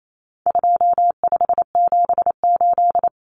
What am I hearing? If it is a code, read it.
Morse code, heard as a keyed tone, about 28 wpm, 700 Hz. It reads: 2578
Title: Common Cut Numbers - 4-Digit 28wpm